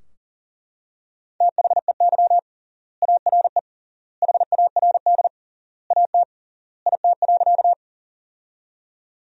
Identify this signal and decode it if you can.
Morse code, heard as a keyed tone, about 40 wpm, 700 Hz. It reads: THEY ARE HARD AT IT.
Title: 3x Word-Spacing - Sentences from Top 300 Words - 40wpm